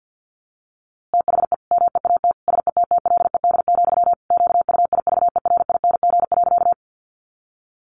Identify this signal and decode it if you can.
Morse code, heard as a keyed tone, about 48 wpm, 700 Hz. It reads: THE MEAT SETTLED, QUIVERING.
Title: A Sound Of Thunder 50wpm